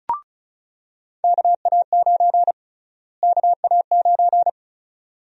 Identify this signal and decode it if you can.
Morse code, heard as a keyed tone, about 36 wpm, 700 Hz. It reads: KA9 KA9
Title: USA Call Sign Prefixes - Repeated 2x - 35wpm